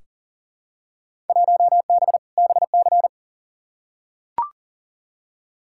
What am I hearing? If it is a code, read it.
Morse code, heard as a keyed tone, about 40 wpm, 700 Hz. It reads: 1B BC
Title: Field Day Exchanges - 40wpm